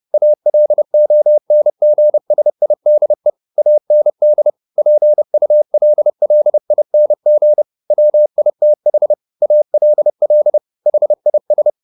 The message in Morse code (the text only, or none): ALONGSIDE AND PULLING WITH ALL HIS